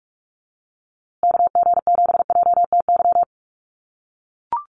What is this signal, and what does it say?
Morse code, heard as a keyed tone, about 44 wpm, 700 Hz. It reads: KG7JTY
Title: USA Call Signs Repeated 2x 45wpm